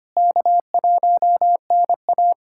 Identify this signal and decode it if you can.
Morse code, heard as a keyed tone, about 25 wpm, 700 Hz. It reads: K1NA
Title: USA Callsign DE Callsign - 4 Characters - 25wpm